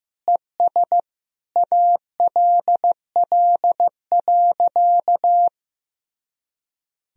Morse code, heard as a keyed tone, about 15 wpm, 700 Hz. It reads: ES ALL.
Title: Sentences from Top 200 Words and Abbreviations 15wpm